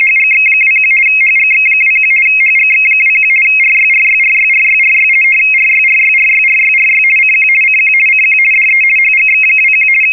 RTTY, about 50 baud. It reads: HGX21 HGX21 HGX21 RYRYRYRYRY RYRYRYRYRY CQ CQ CQ DE HGX21
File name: europe / hng / hng-hngd-mfa budapest-rtty-50 bd-(1986)-(rb)-(mc-914-1-16)